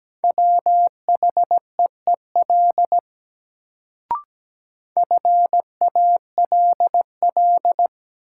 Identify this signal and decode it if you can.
Morse code, heard as a keyed tone, about 17 wpm, 700 Hz. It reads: WHEEL FALL
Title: Sentences from Top 500 Words - New Word Drill - 17wpm